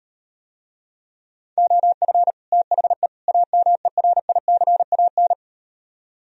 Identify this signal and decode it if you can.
Morse code, heard as a keyed tone, about 38 wpm, 700 Hz. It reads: OF THE AMERICAN